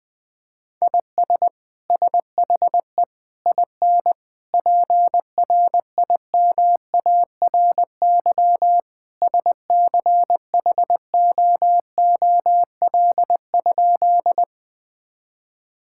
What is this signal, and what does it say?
Morse code, heard as a keyed tone, about 21 wpm, 700 Hz. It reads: IS SHE IN PRIMARY SCHOOL?
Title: Sentences from Top 1000 Words - Review Part 5 - 20wpm